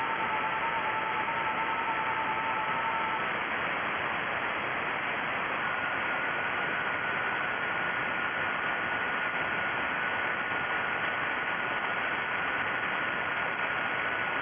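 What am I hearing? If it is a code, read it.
FT8.
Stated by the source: FT8 received on Icom IC-7100.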